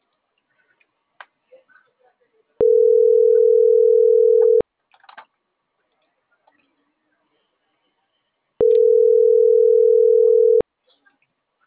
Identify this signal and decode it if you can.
Ringback tone (North America), 440 + 480 Hz.